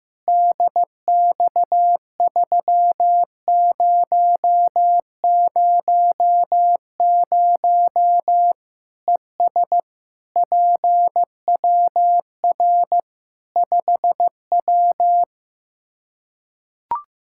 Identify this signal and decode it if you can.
Morse code, heard as a keyed tone, about 16 wpm, 700 Hz. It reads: DX3000 ES PWR 5W
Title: K1USN SST QSO Practice 15wpm